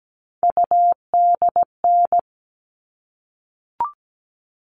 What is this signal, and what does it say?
Morse code, heard as a keyed tone, about 17 wpm, 700 Hz. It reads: UDN
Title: All Cut Numbers - 3-Digit 17wpm